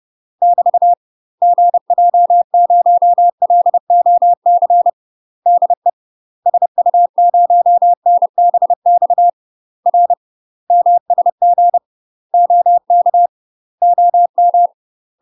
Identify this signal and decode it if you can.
Morse code, heard as a keyed tone, about 30 wpm, 700 Hz. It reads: <BT> GJ0LOC DE SU0NBX R MSG OK OM
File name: UK MASTER QSOs TRAINING / 30WPM / UK MASTER QSOs 0014